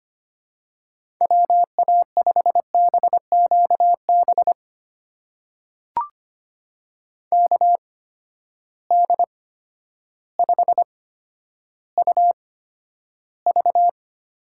Morse code, heard as a keyed tone, about 25 wpm, spaced as for 11 wpm, 700 Hz. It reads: WA5BQB KD5UV
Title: USA Call Signs - 6 Characters - 8x Character Spacing - 25wpm